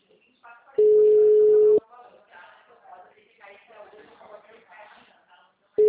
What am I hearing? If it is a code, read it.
Ringback tone (Europe, 425 Hz), with speech.